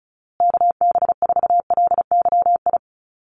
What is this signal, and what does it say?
Morse code, heard as a keyed tone, about 34 wpm, 700 Hz. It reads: KB4LYI
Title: USA Call Signs Repeated 2x 35wpm